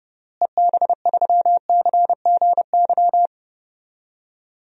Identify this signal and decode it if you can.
Morse code, heard as a keyed tone, about 32 wpm, 700 Hz. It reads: EB3CGY
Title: DXCC-100 Call Signs - 6 Characters - Repeated 3x - 30wpm